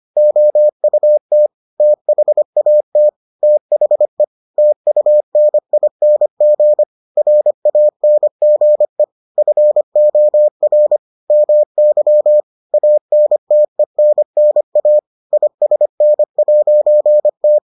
Morse code, heard as a keyed tone, about 25 wpm, 600 Hz. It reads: OUT THAT THE TUNING RANGE FOR MY ANTENNA ISN'T